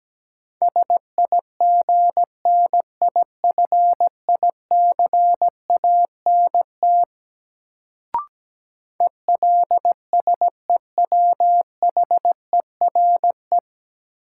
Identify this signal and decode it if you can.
Morse code, heard as a keyed tone, about 17 wpm, 700 Hz. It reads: SIGNIFICANT ELSEWHERE
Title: Sentences from Top 1600 Words - New Word Drill Alt - 17wpm